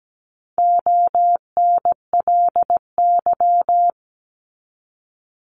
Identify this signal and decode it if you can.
Morse code, heard as a keyed tone, about 17 wpm, 700 Hz. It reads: ONLY
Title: Top 200 Words 17wpm